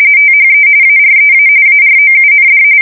RTTY, about 45.45 baud. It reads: THE AREA AROUND
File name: fsk170 / usnavy2